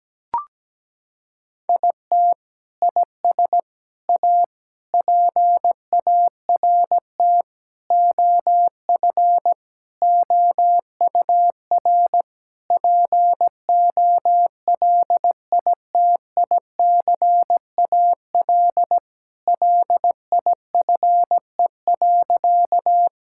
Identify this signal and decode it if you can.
Morse code, heard as a keyed tone, about 17 wpm, 700 Hz. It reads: IT IS A PART OF OUR POLITICAL LIFE.
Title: Sentences from Top 1000 Words - Review Part 5 - 17wpm